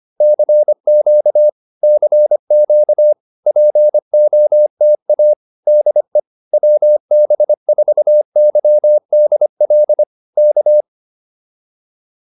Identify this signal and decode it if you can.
Morse code, heard as a keyed tone, about 25 wpm, 600 Hz. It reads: CQ CQ POTA DE WB4YDL K